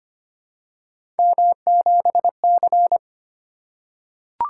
Morse code, heard as a keyed tone, about 25 wpm, 700 Hz. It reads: M7C
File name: Three Letter-Number Combinations - 25wpm